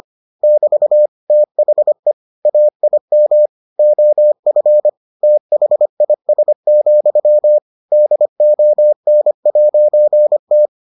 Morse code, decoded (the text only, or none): <BT> THE AIM OF THIS, DON'T